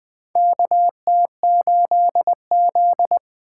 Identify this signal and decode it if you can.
Morse code, heard as a keyed tone, about 20 wpm, 700 Hz. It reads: K T 8 Z
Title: USA Call Signs - 4 Characters - 6x Character Spacing - 20wpm